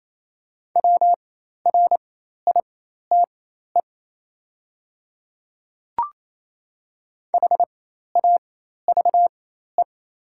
Morse code, heard as a keyed tone, about 28 wpm, spaced as for 13 wpm, 700 Hz. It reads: WRITE HAVE
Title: 4x Character Spacing - Top 100 Words - 28wpm